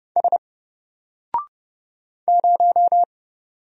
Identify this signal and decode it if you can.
Morse code, heard as a keyed tone, about 30 wpm, 700 Hz. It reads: S 0
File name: Single-Letter-Number-30wpm